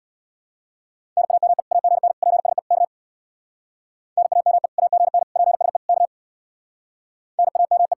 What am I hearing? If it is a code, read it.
Morse code, heard as a keyed tone, about 64 wpm, 700 Hz. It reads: NAME AGN PSE K NAME AGN PSE K NAME